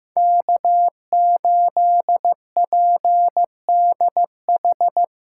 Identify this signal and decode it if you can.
Morse code, heard as a keyed tone, about 15 wpm, 700 Hz. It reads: K8PDH
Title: USA Call Signs - 5 Characters - 4x Character Spacing - 15wpm